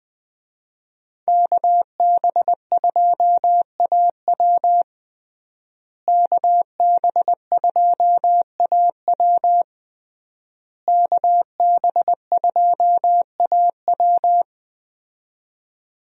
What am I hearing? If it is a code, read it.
Morse code, heard as a keyed tone, about 20 wpm, 700 Hz. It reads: KB2AW KB2AW KB2AW
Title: USA Call Signs - 5 Characters - Repeated 3x - 20wpm